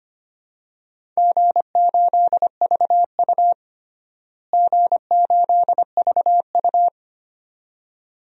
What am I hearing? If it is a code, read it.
Morse code, heard as a keyed tone, about 25 wpm, 700 Hz. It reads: G8VU G8VU
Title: DXCC-100 Call Signs - 4 Characters - Repeated 2x - 25wpm